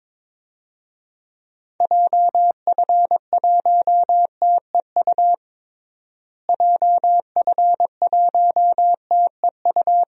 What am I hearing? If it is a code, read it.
Morse code, heard as a keyed tone, about 22 wpm, 700 Hz. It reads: JF1TEU JF1TEU
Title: DXCC-100 Call Signs - 6 Characters - Repeated 2x - 22wpm